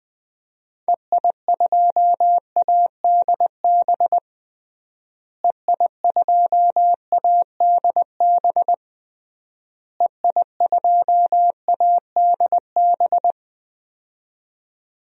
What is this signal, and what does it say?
Morse code, heard as a keyed tone, about 20 wpm, 700 Hz. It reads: EI2ADB EI2ADB EI2ADB
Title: DXCC-100 Call Signs - 6 Characters - Repeated 3x - 20wpm